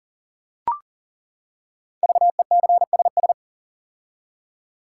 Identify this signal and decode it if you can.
Morse code, heard as a keyed tone, about 40 wpm, 700 Hz. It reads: VECSS
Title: Code Groups - 5 Letters - 40wpm